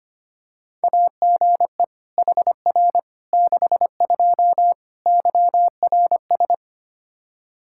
Morse code, heard as a keyed tone, about 25 wpm, 700 Hz. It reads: AGE HR 62 YRS
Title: Practice QSO - Exchange 3 - More Info - 25wpm